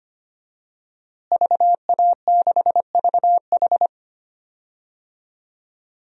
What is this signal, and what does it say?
Morse code, heard as a keyed tone, about 25 wpm, 700 Hz. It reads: VA6VH
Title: DXCC-100 Call Signs - 5 Characters - Repeated 3x - 25wpm